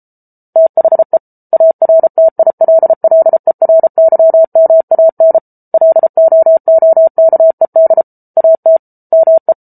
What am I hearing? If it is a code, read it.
Morse code, heard as a keyed tone, about 34 wpm, 650 Hz. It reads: THE ARTILLERYMAN LOOKED AT ME